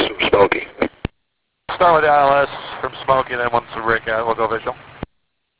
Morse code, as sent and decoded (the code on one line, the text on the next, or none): none